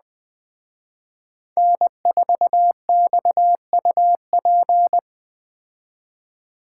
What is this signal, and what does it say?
Morse code, heard as a keyed tone, about 20 wpm, 700 Hz. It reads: N4XUP